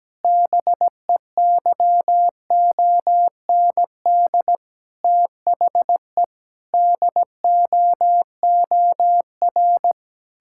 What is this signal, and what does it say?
Morse code, heard as a keyed tone, about 17 wpm, 700 Hz. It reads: BEYOND THE DOOR
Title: Sentences from Top 1000 Words - Review Part 6 - 17wpm